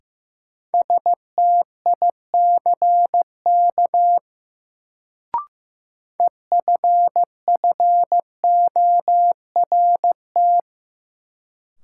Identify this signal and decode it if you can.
Morse code, heard as a keyed tone, about 15 wpm, 700 Hz. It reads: STICK EFFORT
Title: Sentences from Top 1200 Words - New Word Drill - 15wpm